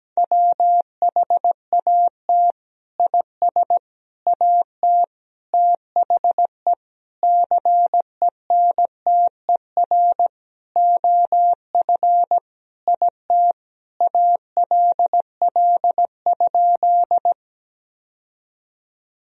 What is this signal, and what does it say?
Morse code, heard as a keyed tone, about 17 wpm, 700 Hz. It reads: WHAT IS AT THE CENTER OF IT ALL?
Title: Sentences from Top 1000 Words - Review Part 2 - 17wpm